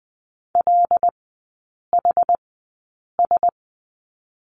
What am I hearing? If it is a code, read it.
Morse code, heard as a keyed tone, about 20 wpm, spaced as for 8 wpm, 700 Hz. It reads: LHS
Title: 074 Lesson 13e - Letter L - Three Characters 20wpm